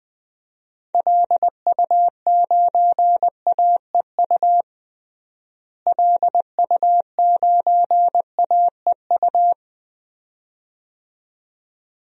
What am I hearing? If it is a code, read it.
Morse code, heard as a keyed tone, about 20 wpm, 700 Hz. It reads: LU9AEU LU9AEU
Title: DXCC-100 Call Signs - 6 Characters - Repeated 2x - 20wpm